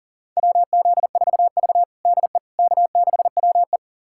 Morse code, heard as a keyed tone, about 40 wpm, 700 Hz. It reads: WZ4V DE K6WE
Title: USA Callsign DE Callsign - 4 Characters - 40wpm